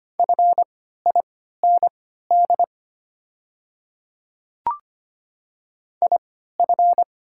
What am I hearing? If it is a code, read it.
Morse code, heard as a keyed tone, about 25 wpm, spaced as for 14 wpm, 700 Hz. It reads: FIND IF
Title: 3x Character Spacing - Top 100 Words - 25wpm